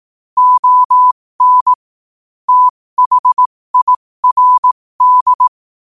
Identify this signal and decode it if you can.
Morse code, heard as a keyed tone, about 18 wpm, spaced as for 15 wpm, 1000 Hz. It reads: ON THIRD
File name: qotdfiles / 15wpm